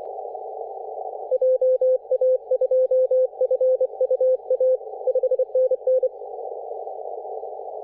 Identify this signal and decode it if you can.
Morse code, heard as a keyed tone, about 25 wpm, 520 Hz. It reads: JA2FUA 5NN
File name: 3D2YA 24.913MHz CW